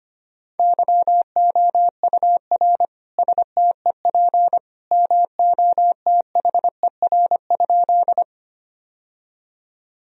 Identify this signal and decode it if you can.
Morse code, heard as a keyed tone, about 25 wpm, 700 Hz. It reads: YOUR STEP MOTHER?
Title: Sentences from Top 1000 Words - Review Part 2 - 25wpm